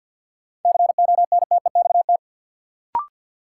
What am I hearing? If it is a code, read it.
Morse code, heard as a keyed tone, about 51 wpm, 700 Hz. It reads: CONTEXT
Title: Sets of 4 Words - New Word Drill - 50wpm